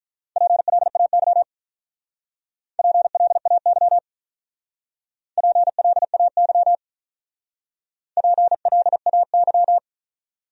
Morse code, heard as a keyed tone, about 45 wpm, 700 Hz. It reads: PLAY PLAY PLAY PLAY